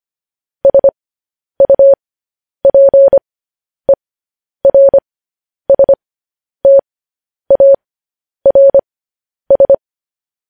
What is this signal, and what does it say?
Morse code, heard as a keyed tone, about 25 wpm, spaced as for 10 wpm, 550 Hz. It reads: SUPERSTARS
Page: news-headlines-in-morse-code-at-10-wpm